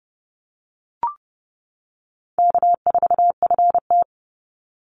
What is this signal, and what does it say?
Morse code, heard as a keyed tone, about 30 wpm, 700 Hz. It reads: K4FT
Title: Easy Call Signs - 30wpm